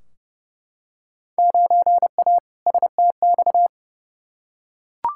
Morse code, heard as a keyed tone, about 30 wpm, 700 Hz. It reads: 9A STX
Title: Field Day Exchanges - 30wpm